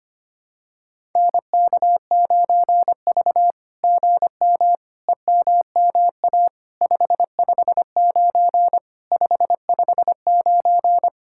Morse code, heard as a keyed tone, about 25 wpm, 700 Hz. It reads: NK9V GM EMMA 559 559
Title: SOTA QSO Practice 25wpm